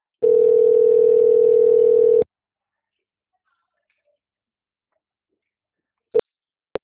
Ringback tone (North America), 440 + 480 Hz.